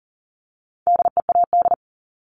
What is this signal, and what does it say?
Morse code, heard as a keyed tone, about 41 wpm, 700 Hz. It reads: DEAD
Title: Multi-letter Phonemes 40wpm